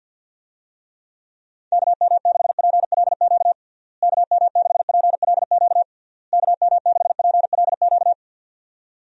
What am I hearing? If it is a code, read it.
Morse code, heard as a keyed tone, about 51 wpm, 700 Hz. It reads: KM6PLQ KM6PLQ KM6PLQ
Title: USA Call Signs - 6 Characters - Repeated 3x - 50wpm